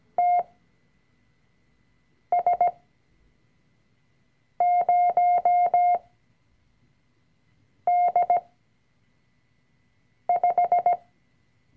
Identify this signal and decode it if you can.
Morse code, heard as a keyed tone, about 17 wpm, spaced as for 4 wpm, 710 Hz. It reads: TS0D5
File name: CW-Online-Kurs / Anfänger / Lektion 005